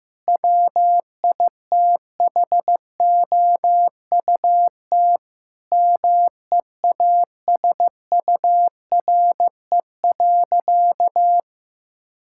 Morse code, heard as a keyed tone, about 15 wpm, 700 Hz. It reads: WITHOUT MEASURE.
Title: Sentences from Top 1000 Words - Review Part 6 - 15wpm